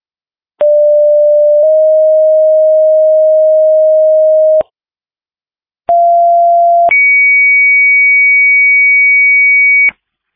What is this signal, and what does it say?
Two-tone page, heard as a one-second tone, then a three-second tone, about 590 then 620 Hz, about 690 then 2250 Hz. It is heard from 0.6 to 9.9 s.
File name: Denver Fire 2026 03 05 19 08 12